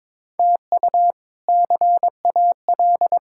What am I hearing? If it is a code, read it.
Morse code, heard as a keyed tone, about 22 wpm, 700 Hz. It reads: TU CAL
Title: K1USN SST QSO Practice - 22wpm